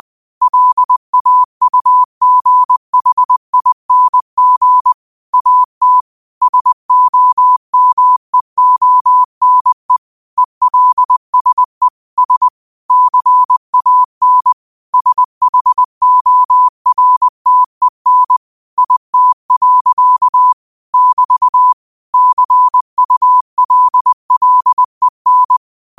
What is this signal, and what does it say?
Morse code, heard as a keyed tone, about 20 wpm, 1000 Hz. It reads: LAUGHING AT SOMEONE ELSE S CAN SHORTEN IT. <BT> CULLEN